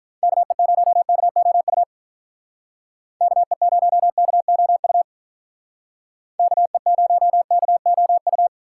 Morse code, heard as a keyed tone, about 46 wpm, 700 Hz. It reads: KE0KOU KE0KOU KE0KOU